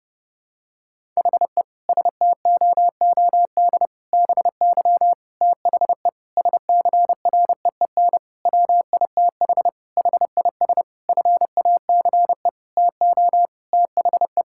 Morse code, heard as a keyed tone, about 30 wpm, 700 Hz. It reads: HE STOOD BY THE SCREEN WITH HIS FACE TO THE
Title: Sentences from Top 1000 Words - Review Part 6 - 30wpm